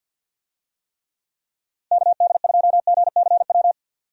Morse code, heard as a keyed tone, about 50 wpm, 700 Hz. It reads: KD2GCW
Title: USA Call Signs - 6 Characters - 50wpm